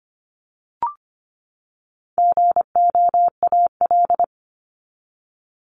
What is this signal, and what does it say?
Morse code, heard as a keyed tone, about 25 wpm, 700 Hz. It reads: GOAL
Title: Top 751-1000 Words 25wpm